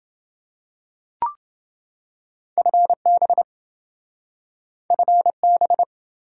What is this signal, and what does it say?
Morse code, heard as a keyed tone, about 28 wpm, 700 Hz. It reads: FB FB